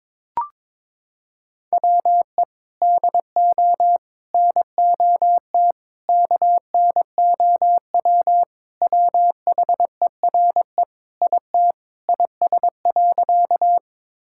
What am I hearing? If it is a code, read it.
Morse code, heard as a keyed tone, about 22 wpm, 700 Hz. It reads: WE DO NOT KNOW WHERE IT IS.
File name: practice / 22wpm / Sentences with 4-Word Phrases - 22wpm